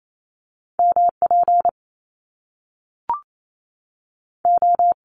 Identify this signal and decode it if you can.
Morse code, heard as a keyed tone, about 28 wpm, spaced as for 3 wpm, 700 Hz. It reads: M PO
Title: Top Two Letter Combinations in Words 28wpm